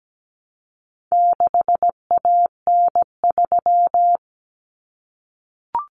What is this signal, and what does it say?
Morse code, heard as a keyed tone, about 17 wpm, 700 Hz. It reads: 6AN3
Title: Common Cut Numbers - 4-Digit 17wpm